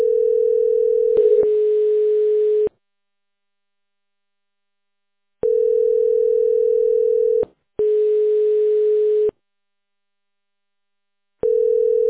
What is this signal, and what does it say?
Ringback tone (North America), 440 + 480 Hz.